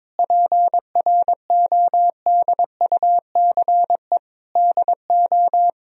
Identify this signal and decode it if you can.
Morse code, heard as a keyed tone, about 22 wpm, 700 Hz. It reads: PRODUCE DO
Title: Sentences from Top 1000 Words - Review Part 3 - 22wpm